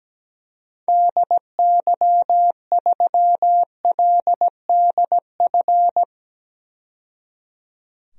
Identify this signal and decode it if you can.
Morse code, heard as a keyed tone, about 17 wpm, 700 Hz. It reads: DY3LDF
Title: DXCC-100 Call Signs - 6 Characters - 17wpm